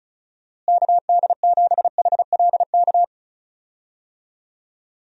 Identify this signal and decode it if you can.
Morse code, heard as a keyed tone, about 36 wpm, 700 Hz. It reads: KD7HLK
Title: USA Call Signs - 6 Characters - Repeated 2x - 35wpm